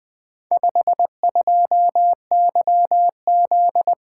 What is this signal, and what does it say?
Morse code, heard as a keyed tone, about 20 wpm, 700 Hz. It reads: 52YZ
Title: Four Letter-Number Combinations - 20wpm